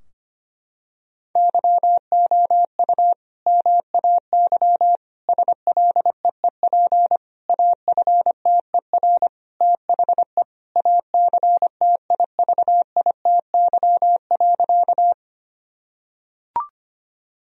Morse code, heard as a keyed tone, about 25 wpm, 700 Hz. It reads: YOU MAY SLEEP AFTER THE ACTIVITY.
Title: Sentences from Top 1000 Words - Review Part 4 - 25wpm